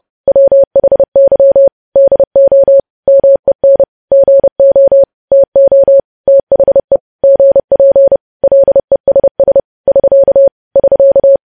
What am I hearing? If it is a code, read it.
Morse code, heard as a keyed tone, about 30 wpm, 550 Hz. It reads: WHY DO MEN GO TO THE GP LESS <SK> <SK>
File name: podcast / News headlines-1x-30x30